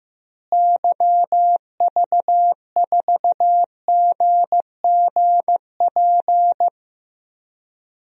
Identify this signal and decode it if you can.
Morse code, heard as a keyed tone, about 15 wpm, 700 Hz. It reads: YV4GGP
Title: DXCC-100 Call Signs - 6 Characters - Repeated 2x - 15wpm